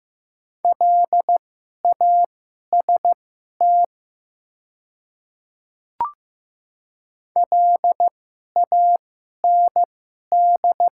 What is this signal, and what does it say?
Morse code, heard as a keyed tone, about 15 wpm, spaced as for 11 wpm, 690 Hz. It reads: LAST LAND
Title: 2x Character Spacing - Top 300 Words - 15wpm